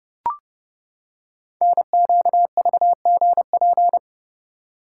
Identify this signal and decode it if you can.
Morse code, heard as a keyed tone, about 29 wpm, 700 Hz. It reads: NQVGP
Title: Code Groups - 5 Letters - 30wpm